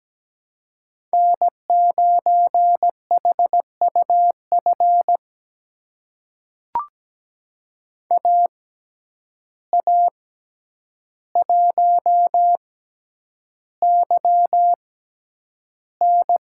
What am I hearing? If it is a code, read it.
Morse code, heard as a keyed tone, about 17 wpm, spaced as for 9 wpm, 700 Hz. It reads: N9HUF AA1YN
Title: USA Call Signs - 5 Characters - 6x Character Spacing - 17wpm